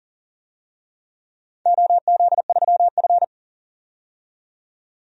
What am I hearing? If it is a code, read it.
Morse code, heard as a keyed tone, about 40 wpm, 700 Hz. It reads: OZ3F
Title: DXCC-100 Call Signs - 4 Characters - 40wpm